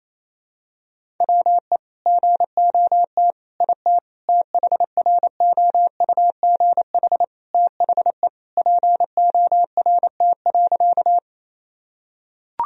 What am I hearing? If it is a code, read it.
Morse code, heard as a keyed tone, about 28 wpm, 700 Hz. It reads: WE GOT IT THROUGH THE PORT.
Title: Sentences from Top 1000 Words - Review Part 1 - 28wpm